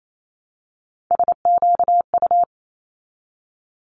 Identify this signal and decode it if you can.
Morse code, heard as a keyed tone, about 28 wpm, 700 Hz. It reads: SQU
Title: Multi-letter Phonemes 28wpm